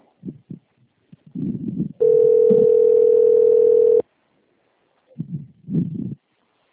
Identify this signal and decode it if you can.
Ringback tone (North America), 440 + 480 Hz.
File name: out- - - - - .7210